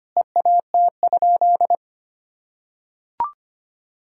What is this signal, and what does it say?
Morse code, heard as a keyed tone, about 25 wpm, 700 Hz. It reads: EAT?
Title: Sentences from Top 300 Words and Abbreviations 25wpm